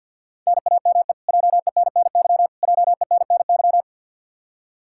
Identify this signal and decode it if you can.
Morse code, heard as a keyed tone, about 51 wpm, 700 Hz. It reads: NAME JENNY JENNY
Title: Practice QSO - Exchange 1 - Beginning - 50wpm